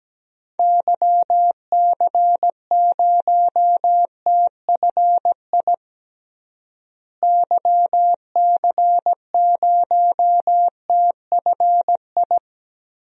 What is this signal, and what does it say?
Morse code, heard as a keyed tone, about 17 wpm, 700 Hz. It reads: YC0TFI YC0TFI
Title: DXCC-100 Call Signs - 6 Characters - Repeated 3x - 17wpm